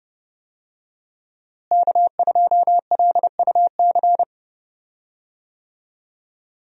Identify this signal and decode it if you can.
Morse code, heard as a keyed tone, about 30 wpm, 700 Hz. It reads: K2LUC
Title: USA Call Signs - 5 Characters - Repeated 3x - 30wpm